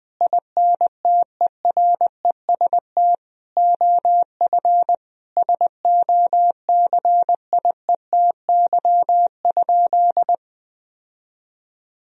Morse code, heard as a keyed tone, about 21 wpm, 700 Hz. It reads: INTEREST OF SOCIETY?
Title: Sentences from Top 1000 Words - Review Part 4 - 20wpm